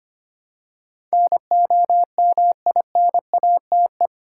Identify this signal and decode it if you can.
Morse code, heard as a keyed tone, about 25 wpm, 700 Hz. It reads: NOMINATE
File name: Eight Letter Words - 25wpm